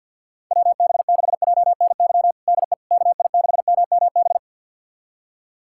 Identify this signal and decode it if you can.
Morse code, heard as a keyed tone, about 51 wpm, 700 Hz. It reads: WB6JNY DE KI6MMB
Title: USA Callsign DE Callsign 50wpm